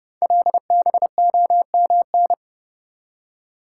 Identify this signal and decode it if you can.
Morse code, heard as a keyed tone, about 29 wpm, 700 Hz. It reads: LBOMN
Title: Code Groups - 5 Letters - 30wpm